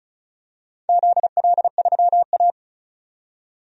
Morse code, heard as a keyed tone, about 34 wpm, 700 Hz. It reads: ZL3A
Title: DXCC-100 Call Signs - 4 Characters - 35wpm